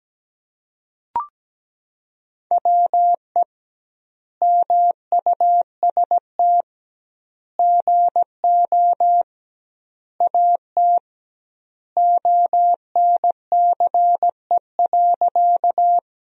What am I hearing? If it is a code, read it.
Morse code, heard as a keyed tone, about 17 wpm, 700 Hz. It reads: WE MUST GO AT ONCE.
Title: 2x Word-Spacing - Sentences from Top 300 Words - 17wpm